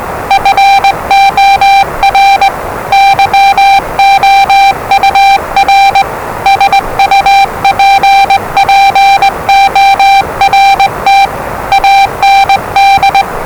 Morse code, heard as a keyed tone, about 20 wpm, 790 Hz, with noise. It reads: FOR YOUR SUPPORT AND
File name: WellingtonRadioZlwCloseDownOnHfAndMf / 01-Zlw 4277khz 417.5 qrt until 0925z